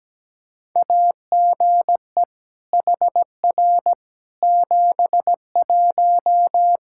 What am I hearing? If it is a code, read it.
Morse code, heard as a keyed tone, about 17 wpm, 700 Hz. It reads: AGE HR 71
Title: Practice QSO - Exchange 3 - More Info - 17wpm